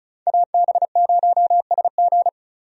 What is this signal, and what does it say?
Morse code, heard as a keyed tone, about 36 wpm, 700 Hz. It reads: AB0SG
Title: USA Call Signs - 5 Characters - 35wpm